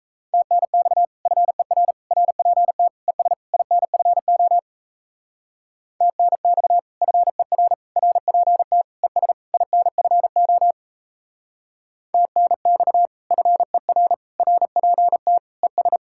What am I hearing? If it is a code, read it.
Morse code, heard as a keyed tone, about 35 wpm, 700 Hz. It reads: TNX FER RPT ES INFO TNX FER RPT ES INFO TNX FER RPT ES
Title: Speed-Racing Common QSO Phrases - 28wpm